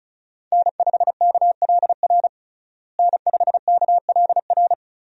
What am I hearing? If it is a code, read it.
Morse code, heard as a keyed tone, about 34 wpm, 700 Hz. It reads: N5KLR N5KLR
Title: USA Call Signs - 5 Characters - Repeated 3x - 35wpm